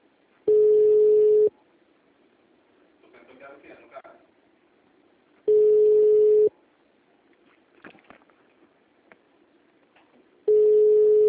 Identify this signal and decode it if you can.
Ringback tone (Europe, 425 Hz), with speech.